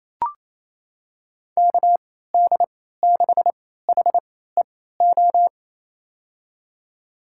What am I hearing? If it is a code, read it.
Morse code, heard as a keyed tone, about 28 wpm, spaced as for 16 wpm, 700 Hz. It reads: KD6HEO
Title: USA Call Signs - 3x Character Spacing - 28wpm